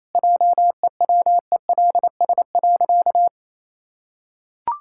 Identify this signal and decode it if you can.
Morse code, heard as a keyed tone, about 28 wpm, 700 Hz. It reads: JEWELS.
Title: Pangram Challenge v2 - 28wpm